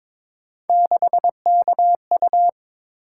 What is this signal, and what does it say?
Morse code, heard as a keyed tone, about 22 wpm, 700 Hz. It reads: 6KU
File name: practice / 22wpm / Three Letter-Number Combinations - 22wpm